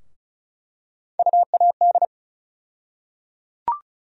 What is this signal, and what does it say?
Morse code, heard as a keyed tone, about 34 wpm, 700 Hz. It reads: UAD
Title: USA Call Sign Postfixes - 3 Letters 35wpm